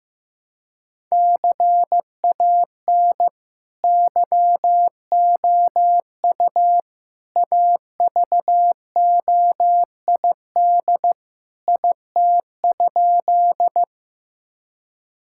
Morse code, heard as a keyed tone, about 15 wpm, 700 Hz. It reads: CAN YOU AVOID IT?
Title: Sentences from Top 1000 Words - Review Part 6 - 15wpm